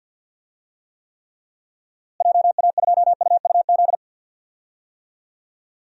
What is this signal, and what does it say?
Morse code, heard as a keyed tone, about 51 wpm, 700 Hz. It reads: JA2UUZ
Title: DXCC-100 Call Signs - 6 Characters - Repeated 3x - 50wpm